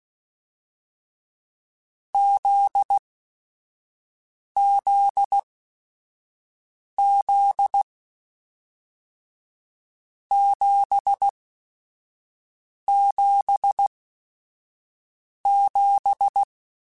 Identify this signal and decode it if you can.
Morse code, heard as a keyed tone, about 16 wpm, spaced as for 5 wpm, 800 Hz. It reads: ZZZ 777
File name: Groups@5 / Groups@5 mp3s / JZ17@5